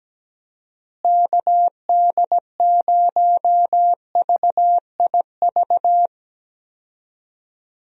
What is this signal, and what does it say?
Morse code, heard as a keyed tone, about 17 wpm, 700 Hz. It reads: KD0VIV
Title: USA Call Signs - 6 Characters - Repeated 3x - 17wpm